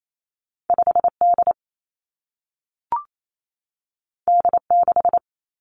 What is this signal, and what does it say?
Morse code, heard as a keyed tone, about 28 wpm, 700 Hz. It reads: 5 D D 6
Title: Two Challenging Letter-Number Combinations 28wpm